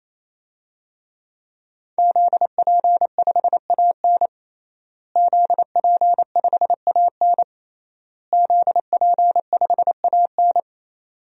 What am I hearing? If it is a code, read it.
Morse code, heard as a keyed tone, about 28 wpm, 700 Hz. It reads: ZP5AN ZP5AN ZP5AN
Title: DXCC-100 Call Signs - 5 Characters - Repeated 3x - 28wpm